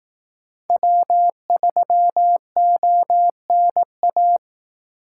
Morse code, heard as a keyed tone, about 18 wpm, 700 Hz. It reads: W3ONA
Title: Speed Racing - USA 1x3 Call Signs 15wpm